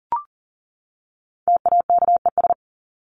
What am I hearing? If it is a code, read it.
Morse code, heard as a keyed tone, about 40 wpm, 700 Hz. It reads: TAKES
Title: ES:S Suffix - Top 100 Words with Base Words 40wpm